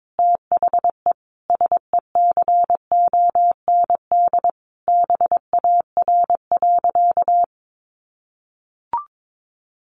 Morse code, heard as a keyed tone, about 22 wpm, 700 Hz. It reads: THE SECOND BAR.
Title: Easy Phrases 22wpm